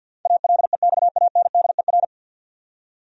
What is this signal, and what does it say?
Morse code, heard as a keyed tone, about 53 wpm, 700 Hz. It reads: ALEXANDER
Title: USA Common Names - 50wpm